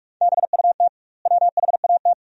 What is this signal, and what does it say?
Morse code, heard as a keyed tone, about 42 wpm, 700 Hz. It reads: BUT WHAT
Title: 2x Word-Spacing - Sentences from Top 200 Words - 45wpm